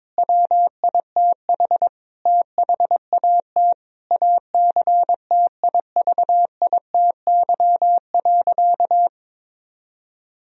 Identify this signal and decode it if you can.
Morse code, heard as a keyed tone, about 22 wpm, 700 Hz. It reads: WITH THAT ACTIVITY.
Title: Sentences from Top 1000 Words - Review Part 4 - 22wpm